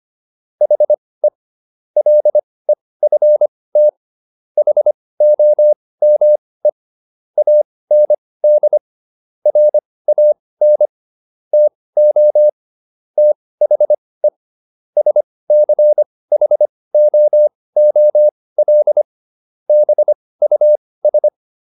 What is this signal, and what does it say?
Morse code, heard as a keyed tone, about 25 wpm, spaced as for 18 wpm, 600 Hz. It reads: HE LEFT HOME AND RAN TO THE SCHOOL BUS